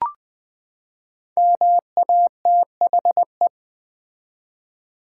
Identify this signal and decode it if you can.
Morse code, heard as a keyed tone, about 21 wpm, 700 Hz. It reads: MATHE
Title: Build A Word - 5 Letters At A Time - 12-Letter Words - 20wpm